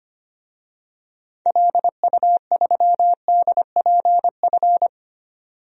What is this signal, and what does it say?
Morse code, heard as a keyed tone, about 25 wpm, 700 Hz. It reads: LU3DPF
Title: DXCC-100 Call Signs - 6 Characters - 25wpm